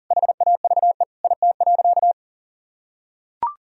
Morse code, heard as a keyed tone, about 40 wpm, 700 Hz. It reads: HAVE IT.
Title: Sentences from Top 1000 Words - Review Part 3 - 40wpm